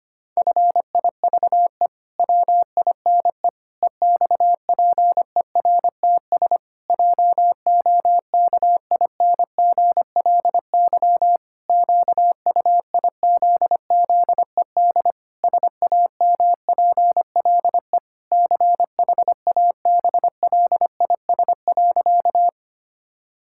Morse code, heard as a keyed tone, about 25 wpm, 700 Hz. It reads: FIVE WINE EXPERTS JOKINGLY QUIZZED SAMPLE CHABLIS.
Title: Pangram Challenge v2 - 25wpm